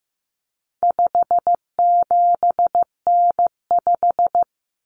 Morse code, heard as a keyed tone, about 15 wpm, 700 Hz. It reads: 57N5
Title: Common Cut Numbers - 4-Digit 15wpm